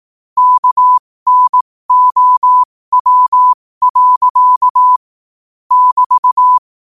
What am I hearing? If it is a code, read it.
Morse code, heard as a keyed tone, about 18 wpm, spaced as for 15 wpm, 1000 Hz. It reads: KNOW. <BT>
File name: qotdfiles / 15wpm